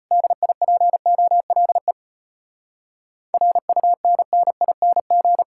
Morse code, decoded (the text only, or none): DIPOLE RUNNING